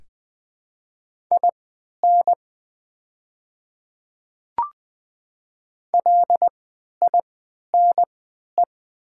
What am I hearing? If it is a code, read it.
Morse code, heard as a keyed tone, about 20 wpm, spaced as for 11 wpm, 700 Hz. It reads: IN LINE
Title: 3x Character Spacing - Top 200 Words - 20wpm